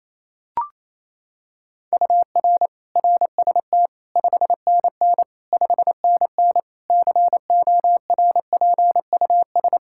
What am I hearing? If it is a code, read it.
Morse code, heard as a keyed tone, about 28 wpm, 700 Hz. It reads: UR RST 5NN 5NN CORPUS
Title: K1USN SST QSO Practice 28wpm